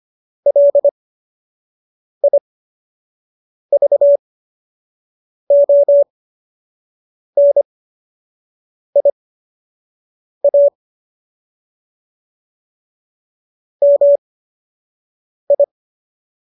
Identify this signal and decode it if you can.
Morse code, heard as a keyed tone, about 25 wpm, spaced as for 6 wpm, 570 Hz. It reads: LIVONIA MI